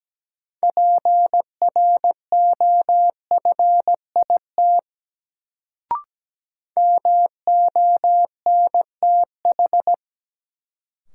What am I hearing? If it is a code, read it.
Morse code, heard as a keyed tone, about 17 wpm, 700 Hz. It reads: PROFIT MONTH
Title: Sentences from Top 800 Words - New Word Drill - 17wpm